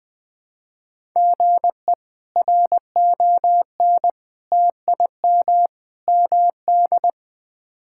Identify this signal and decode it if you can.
Morse code, heard as a keyed tone, about 20 wpm, 700 Hz. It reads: GE RON TIM MD
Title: K1USN SST QSO Practice - 20wpm